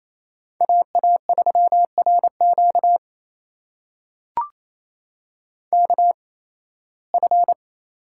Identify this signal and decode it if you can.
Morse code, heard as a keyed tone, about 28 wpm, spaced as for 18 wpm, 700 Hz. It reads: AA3RQ KF
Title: USA Call Signs - 5 Characters - 8x Character Spacing - 28wpm